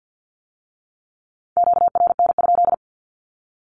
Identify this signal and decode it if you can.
Morse code, heard as a keyed tone, about 50 wpm, 700 Hz. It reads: QRN?
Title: Q-Codes with Spoken Meaning - 50wpm